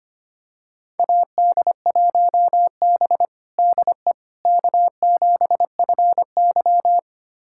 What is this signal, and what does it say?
Morse code, heard as a keyed tone, about 25 wpm, 700 Hz. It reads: AD1B DE K7FY
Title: USA Callsign DE Callsign - 4 Characters - 25wpm